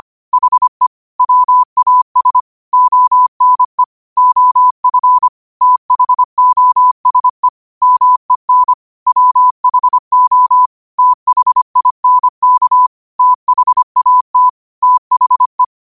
Morse code, heard as a keyed tone, about 25 wpm, 1000 Hz. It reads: HE WAS ONE OF THOSE MEN WHO THINK THAT THE
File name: qotdfiles / 25wpm